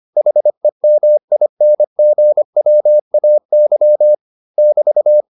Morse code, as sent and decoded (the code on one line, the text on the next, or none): .... -- .. -. --. .-- .- -.-- / -...-
HEMINGWAY <BT>